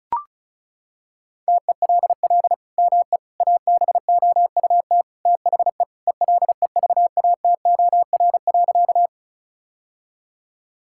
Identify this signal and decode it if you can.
Morse code, heard as a keyed tone, about 36 wpm, 700 Hz. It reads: TELL ME ABOUT THE ELEVATOR.
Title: Sentences from Top 1600 Words 35wpm